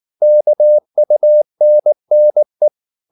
Morse code, heard as a keyed tone, about 19 wpm, 600 Hz. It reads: KUNNE
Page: Ferie 19 wpm